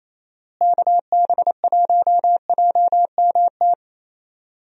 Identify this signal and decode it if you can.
Morse code, heard as a keyed tone, about 27 wpm, 700 Hz. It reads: KB1JMT
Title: USA Call Signs - 6 Characters - Repeated 3x - 28wpm